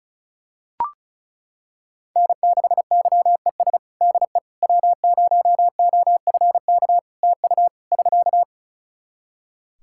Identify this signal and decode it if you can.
Morse code, heard as a keyed tone, about 36 wpm, 700 Hz. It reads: N6YES DE W0OFK TU <SK>
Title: K1USN SST QSO Practice 35wpm